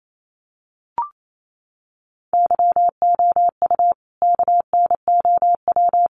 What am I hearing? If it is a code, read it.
Morse code, heard as a keyed tone, about 28 wpm, 700 Hz. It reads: YOU KNOW
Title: Sentences from Top 300 Words 28wpm